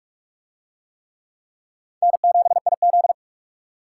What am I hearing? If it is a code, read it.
Morse code, heard as a keyed tone, about 40 wpm, 700 Hz. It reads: N7IZ